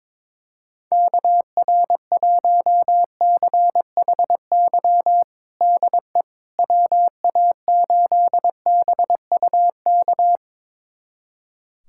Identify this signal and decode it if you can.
Morse code, heard as a keyed tone, about 22 wpm, 700 Hz. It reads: KR1CHY DE WA8BUK
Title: USA Callsign DE Callsign - 6 Characters - 22wpm